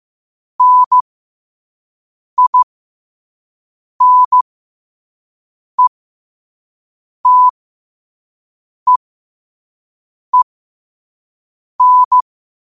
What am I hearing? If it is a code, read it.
Morse code, heard as a keyed tone, about 15 wpm, spaced as for 5 wpm, 1000 Hz. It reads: NINETEEN
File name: qotdfiles / 5wpm